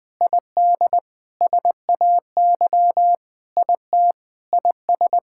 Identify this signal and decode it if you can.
Morse code, heard as a keyed tone, about 21 wpm, 700 Hz. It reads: ID SAY IT IS
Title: Sentences with 6-Word Phrases - Spoken Beginning - 20wpm